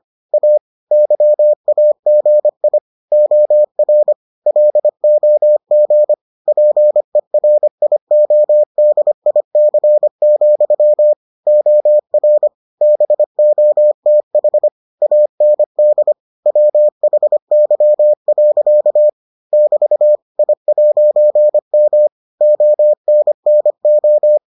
Morse code, heard as a keyed tone, about 25 wpm, 600 Hz. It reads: A YAGI OR LOG PERIODIC, OR BOTH AND WHY. <BT> I'M ONNO